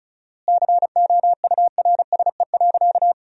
Morse code, heard as a keyed tone, about 36 wpm, 700 Hz. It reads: COURSE.
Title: 4x Word-Spacing - Sentences from Top 500 Words - 35wpm